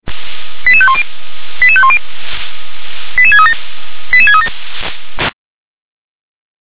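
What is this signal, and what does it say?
Five-tone selective call (ZVEI), heard as quick runs of five tones, tones of about 70 ms. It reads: 88510 88510 88637 88637
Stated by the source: Five-tone ZVEI selective call received by SDR.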